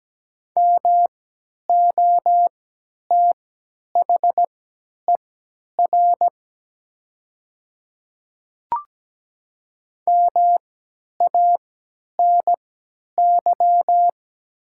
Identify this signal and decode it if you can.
Morse code, heard as a keyed tone, about 17 wpm, spaced as for 10 wpm, 700 Hz. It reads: MOTHER MANY
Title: 3x Character Spacing - Top 500 Words - 17wpm